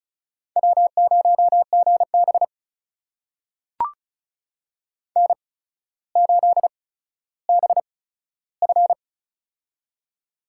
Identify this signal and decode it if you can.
Morse code, heard as a keyed tone, about 36 wpm, spaced as for 15 wpm, 700 Hz. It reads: W0GB N8BF
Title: USA Call Signs - 4 Characters - 8x Character Spacing - 35wpm